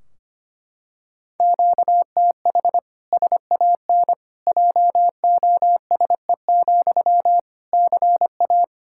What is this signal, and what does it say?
Morse code, heard as a keyed tone, about 25 wpm, 700 Hz. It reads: QTH SAN JOSE, CA
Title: Practice QSO - Exchange 1 - Beginning - 25wpm